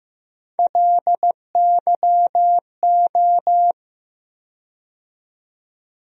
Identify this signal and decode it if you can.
Morse code, heard as a keyed tone, about 15 wpm, 700 Hz. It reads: LYO
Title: USA Call Sign Postfixes - 3 Letters 15wpm